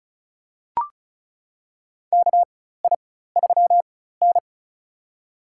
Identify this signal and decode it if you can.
Morse code, heard as a keyed tone, about 36 wpm, spaced as for 16 wpm, 700 Hz. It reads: KI3N
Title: USA Call Signs - 4x Character Spacing - 35wpm